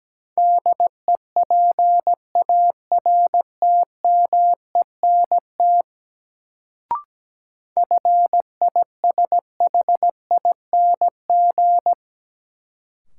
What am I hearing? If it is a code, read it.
Morse code, heard as a keyed tone, about 17 wpm, 700 Hz. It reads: DEPARTMENT FISHING
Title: Sentences from Top 1000 Words - Review Part 5 - New Word Drill - 17wpm